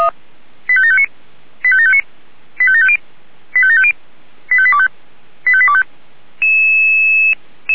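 Five-tone selective call (ZVEI), heard as quick runs of five tones, tones of about 70 ms. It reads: 85759 85759 85750 85750 85825 85825
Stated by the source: Five-tone ZVEI selective call 85730, received by SDR.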